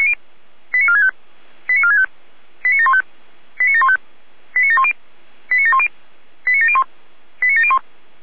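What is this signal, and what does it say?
Five-tone selective call (ZVEI), heard as quick runs of five tones, tones of about 70 ms. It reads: 89465 89465 89715 89715 89710 89710 89701 89701
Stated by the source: ZVEI five-tone selective call received on an SDR.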